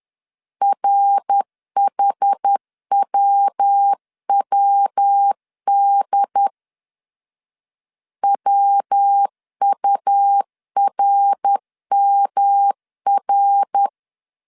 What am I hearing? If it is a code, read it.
Morse code, heard as a keyed tone, about 10 wpm, 800 Hz. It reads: RHWWD WURMR
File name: CW / DET / 1 3 1500C S B 10wpm 3335